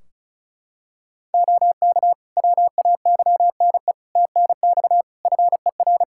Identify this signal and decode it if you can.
Morse code, heard as a keyed tone, about 34 wpm, 700 Hz. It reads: OK WAYNE TNX FER
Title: Practice QSO - Exchange 4 - Ending - 35wpm